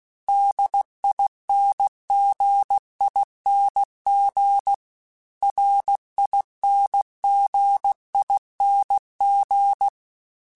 Morse code, heard as a keyed tone, about 16 wpm, 800 Hz. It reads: DINGING RINGING
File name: Special INGs-mp3s / Special INGs-mp3s / Dble Ings @15